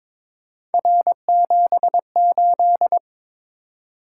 Morse code, heard as a keyed tone, about 22 wpm, 700 Hz. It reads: R78
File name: practice / 22wpm / Three Letter-Number Combinations - 22wpm